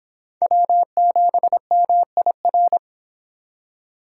Morse code, heard as a keyed tone, about 24 wpm, 700 Hz. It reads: W7MIR